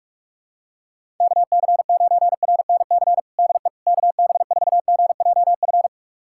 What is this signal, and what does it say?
Morse code, heard as a keyed tone, about 46 wpm, 700 Hz. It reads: KC9RNC DE KB4GJF
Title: USA Callsign DE Callsign 45wpm